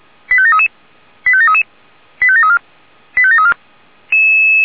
Five-tone selective call (ZVEI), heard as quick runs of five tones, tones of about 70 ms. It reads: 85733 85733 85734 85734